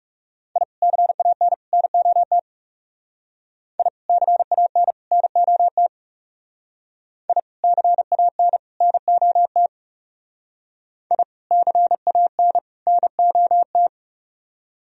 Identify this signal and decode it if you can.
Morse code, heard as a keyed tone, about 36 wpm, 700 Hz. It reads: I CAN NOT I CAN NOT I CAN NOT I CAN NOT
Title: Speed-Racing - Sets of 3 Words 30wpm